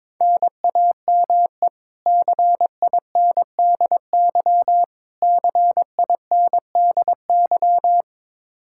Morse code, heard as a keyed tone, about 22 wpm, 700 Hz. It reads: NAME CINDY CINDY
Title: K1USN SST QSO Practice 22wpm